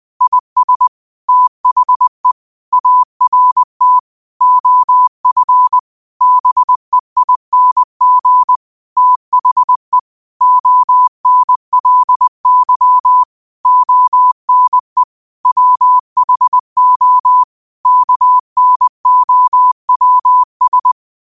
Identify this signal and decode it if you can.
Morse code, heard as a keyed tone, about 20 wpm, 1000 Hz. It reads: IS THE ART OF BEING THE ONLY ONE WHO KNOWS
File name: qotdfiles / 20wpm